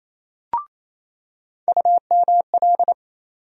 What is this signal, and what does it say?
Morse code, heard as a keyed tone, about 28 wpm, 700 Hz. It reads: UML
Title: USA Call Sign Postfixes - 3 Letters 28wpm